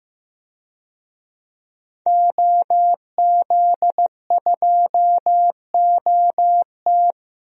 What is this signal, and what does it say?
Morse code, heard as a keyed tone, about 15 wpm, 700 Hz. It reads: OZ2OT
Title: DXCC-100 Call Signs - 5 Characters - 15wpm